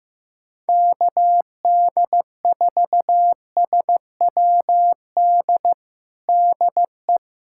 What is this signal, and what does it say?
Morse code, heard as a keyed tone, about 15 wpm, 700 Hz. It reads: KD4SWD DE
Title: USA Callsign DE Callsign - 6 Characters - 15wpm